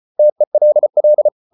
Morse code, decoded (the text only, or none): TELL